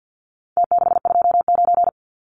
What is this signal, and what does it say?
Morse code, heard as a keyed tone, about 49 wpm, 700 Hz. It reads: T629
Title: Numbers - Contest Style - 4-Digit 50wpm